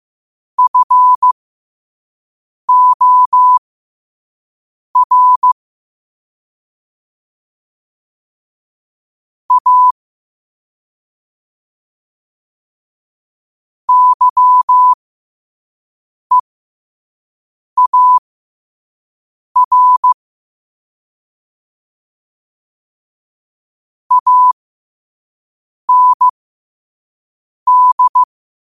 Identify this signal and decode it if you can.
Morse code, heard as a keyed tone, about 15 wpm, spaced as for 5 wpm, 1000 Hz. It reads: FOR A YEAR AND
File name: qotdfiles / 5wpm